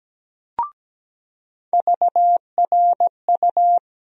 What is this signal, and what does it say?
Morse code, heard as a keyed tone, about 17 wpm, 700 Hz. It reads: VRU
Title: USA Call Sign Postfixes - 3 Letters 17wpm